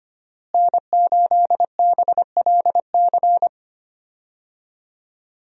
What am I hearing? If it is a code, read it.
Morse code, heard as a keyed tone, about 25 wpm, 700 Hz. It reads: N8BLC
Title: USA Call Signs - 5 Characters - Repeated 2x - 25wpm